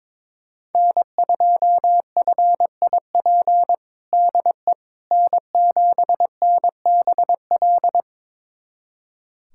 Morse code, heard as a keyed tone, about 22 wpm, 700 Hz. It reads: N2FIP DE N7NBL
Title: USA Callsign DE Callsign - 5 Characters - 22wpm